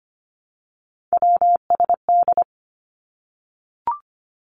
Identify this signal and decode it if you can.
Morse code, heard as a keyed tone, about 25 wpm, 700 Hz. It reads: WSD
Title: Three Challenging Letter-Number Combinations 25wpm